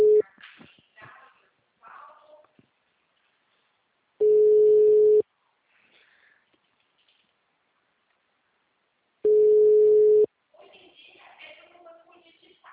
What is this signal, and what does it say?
Ringback tone (Europe, 425 Hz).
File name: out- -2027- - - .3384